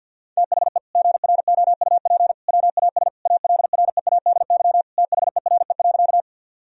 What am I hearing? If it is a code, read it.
Morse code, heard as a keyed tone, about 51 wpm, 700 Hz. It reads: THE GROUP WAS ALREADY THERE.
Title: Sentences from Top 1000 Words - Review Part 1 - 50wpm